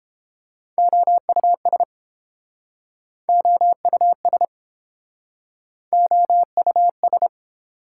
Morse code, heard as a keyed tone, about 30 wpm, 700 Hz. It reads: OUS OUS OUS